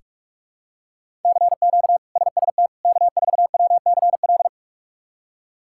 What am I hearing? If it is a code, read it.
Morse code, heard as a keyed tone, about 46 wpm, 700 Hz. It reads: CQ SST K4WCL
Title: K1USN SST QSO Practice - 45wpm